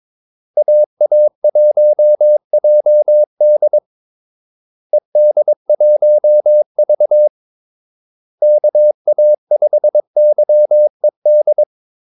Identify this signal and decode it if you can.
Morse code, heard as a keyed tone, about 22 wpm, 600 Hz. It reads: AA1JD ED1V KA5YED
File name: Callsigns-22wpm